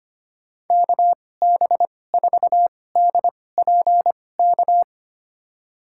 Morse code, heard as a keyed tone, about 25 wpm, spaced as for 18 wpm, 700 Hz. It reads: KB4DPK
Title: USA Call Signs - 2x Character Spacing - 25wpm